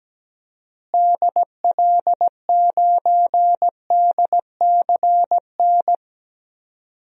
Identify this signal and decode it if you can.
Morse code, heard as a keyed tone, about 17 wpm, 700 Hz. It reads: DL9DCN
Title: DXCC-100 Call Signs - 6 Characters - Repeated 3x - 17wpm